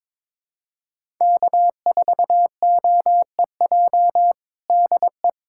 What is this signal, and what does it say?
Morse code, heard as a keyed tone, about 22 wpm, 700 Hz. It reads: K4OEJ DE
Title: USA Callsign DE Callsign - 5 Characters - 22wpm